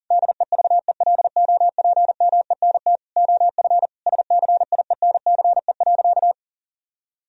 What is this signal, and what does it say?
Morse code, heard as a keyed tone, about 37 wpm, 700 Hz. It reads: DEVELOPMENT OF SCIENCE.
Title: Sentences from Top 1000 Words - Review Part 4 - 40wpm